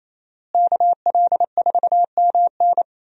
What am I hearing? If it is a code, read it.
Morse code, heard as a keyed tone, about 28 wpm, 700 Hz. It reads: KL4MN
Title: USA Call Signs - 5 Characters - Repeated 2x - 28wpm